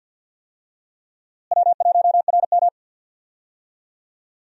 Morse code, heard as a keyed tone, about 52 wpm, 700 Hz. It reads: W1RM
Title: USA Call Signs - 4 Characters - Repeated 3x - 50wpm